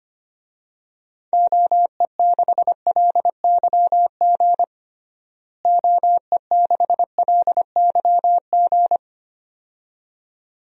Morse code, heard as a keyed tone, about 25 wpm, 700 Hz. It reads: OE6LYG OE6LYG
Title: DXCC-100 Call Signs - 6 Characters - Repeated 2x - 25wpm